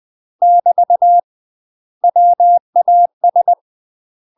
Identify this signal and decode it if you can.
Morse code, heard as a keyed tone, about 20 wpm, 700 Hz. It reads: <BT> WAS
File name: UK MASTER QSOs TRAINING / 20WPM / UK MASTER QSOs 0001